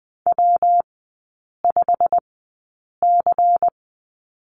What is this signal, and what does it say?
Morse code, heard as a keyed tone, about 20 wpm, spaced as for 8 wpm, 700 Hz. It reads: W5C
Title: 114 Lesson 19e - Letter w - Three Characters 20wpm